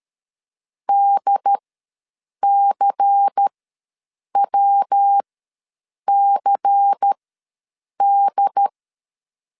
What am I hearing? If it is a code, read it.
Morse code, heard as a keyed tone, about 13 wpm, spaced as for 7 wpm, 800 Hz. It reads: DCWCD
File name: LEZIONI / 5 LBVC / 5 LBVC ws 12wpm 3159